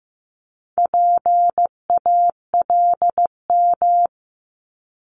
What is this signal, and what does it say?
Morse code, heard as a keyed tone, about 15 wpm, 700 Hz. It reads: PALM
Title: Multi-letter Phonemes Repeated 3x 15wpm